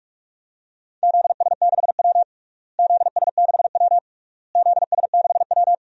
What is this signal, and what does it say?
Morse code, heard as a keyed tone, about 47 wpm, 700 Hz. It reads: ZS6W ZS6W ZS6W
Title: DXCC-100 Call Signs - 4 Characters - Repeated 3x - 45wpm